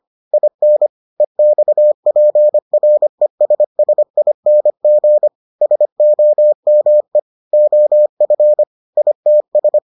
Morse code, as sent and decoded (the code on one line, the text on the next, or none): .. -. / . -..- .--. .-. ... ... .. -. --. / ... --- -- . / --- ..-. / .. - ...
IN EXPRESSING SOME OF ITS